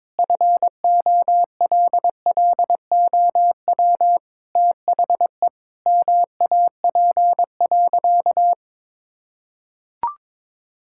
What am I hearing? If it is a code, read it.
Morse code, heard as a keyed tone, about 22 wpm, 700 Hz. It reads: FOLLOW THE MAP.
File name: practice / 22wpm / Sentences with 8-Word Phrases - 22wpm